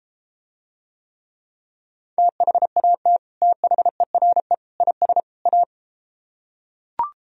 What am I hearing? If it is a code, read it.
Morse code, heard as a keyed tone, about 33 wpm, 700 Hz. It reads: THAT THERE IS A